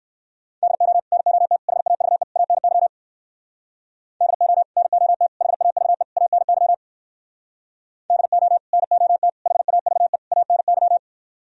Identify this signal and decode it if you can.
Morse code, heard as a keyed tone, about 67 wpm, 700 Hz. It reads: DO NOT HAVE ANY DO NOT HAVE ANY DO NOT HAVE ANY